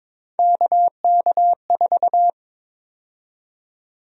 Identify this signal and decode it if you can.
Morse code, heard as a keyed tone, about 23 wpm, 700 Hz. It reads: KK4
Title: USA Call Sign Prefixes - 22wpm